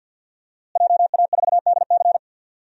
Morse code, heard as a keyed tone, about 52 wpm, 700 Hz. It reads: JA4DC
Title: DXCC-100 Call Signs - 5 Characters - Repeated 2x - 50wpm